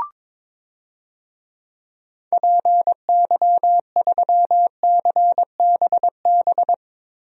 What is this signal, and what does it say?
Morse code, heard as a keyed tone, about 22 wpm, 700 Hz. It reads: PY3CBB
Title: DXCC-100 Call Signs - 6 Characters - 22wpm